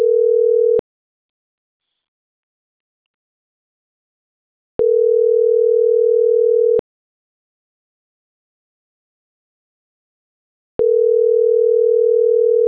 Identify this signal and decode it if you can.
Ringback tone (North America), 440 + 480 Hz.